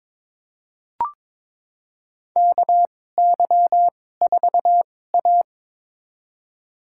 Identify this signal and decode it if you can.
Morse code, heard as a keyed tone, about 22 wpm, spaced as for 16 wpm, 700 Hz. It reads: KY4A
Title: USA Call Signs - 4 Characters - 2x Character Spacing - 22wpm